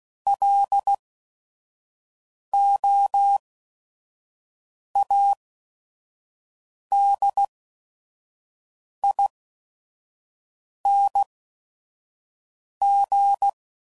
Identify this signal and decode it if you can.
Morse code, heard as a keyed tone, about 16 wpm, spaced as for 5 wpm, 800 Hz. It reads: LOADING
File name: Prefixes / By First Letter / Ls